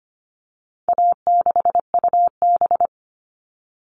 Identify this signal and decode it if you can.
Morse code, heard as a keyed tone, about 25 wpm, 700 Hz. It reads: A6UB
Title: All Cut Numbers - 4-Digit 25wpm